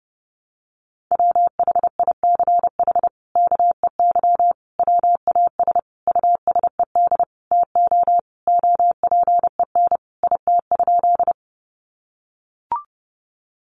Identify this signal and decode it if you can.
Morse code, heard as a keyed tone, about 30 wpm, 700 Hz. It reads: WHICH KEY WAS USED TO OPEN IT?
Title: Sentences from Top 600 Words 30wpm